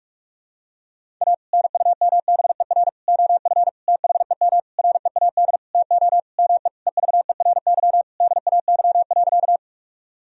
Morse code, heard as a keyed tone, about 46 wpm, 700 Hz. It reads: A NUMBER OF THEM READ TO ME EVERY DAY.
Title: Sentences from Top 1000 Words - Review Part 1 - 45wpm